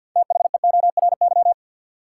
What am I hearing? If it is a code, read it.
Morse code, heard as a keyed tone, about 53 wpm, 700 Hz. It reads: THEORY
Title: Sentences from Top 600 Words - New Word Drill - 50wpm